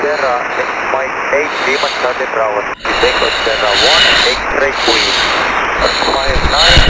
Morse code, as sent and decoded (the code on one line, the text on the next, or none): none